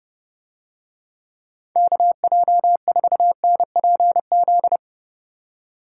Morse code, heard as a keyed tone, about 32 wpm, 700 Hz. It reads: KJ4NPZ